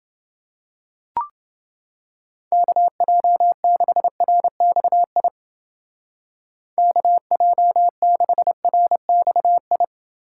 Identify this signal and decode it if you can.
Morse code, heard as a keyed tone, about 31 wpm, 700 Hz. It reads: KJ6RXI KJ6RXI